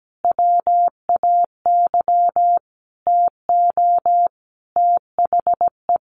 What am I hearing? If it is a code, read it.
Morse code, heard as a keyed tone, about 17 wpm, 700 Hz. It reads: WAY TO THE
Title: Sets of 5 Words Encore2 17wpm